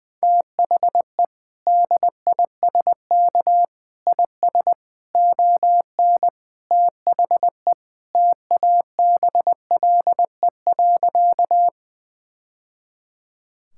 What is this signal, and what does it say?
Morse code, heard as a keyed tone, about 20 wpm, 700 Hz. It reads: THE DISK IS ON THE TABLE.
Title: Sentences from Top 1000 Words - Review Part 6 - 20wpm